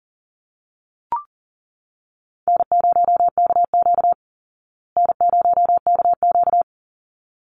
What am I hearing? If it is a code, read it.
Morse code, heard as a keyed tone, about 40 wpm, 700 Hz. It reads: N0KQ N0KQ
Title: Easy Call Signs Repeated 2x - 40wpm